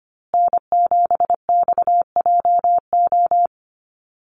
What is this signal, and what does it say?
Morse code, heard as a keyed tone, about 25 wpm, 700 Hz. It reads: N7XJO
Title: USA Call Signs Repeated 2x 25wpm